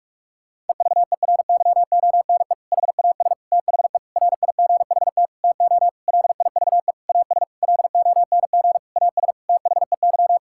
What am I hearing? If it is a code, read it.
Morse code, heard as a keyed tone, about 46 wpm, 700 Hz. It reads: EVERYONE HAS THE RIGHT TO LIVE AS LONG AS THEY
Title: Sentences from Top 1000 Words - Review Part 5 - 45wpm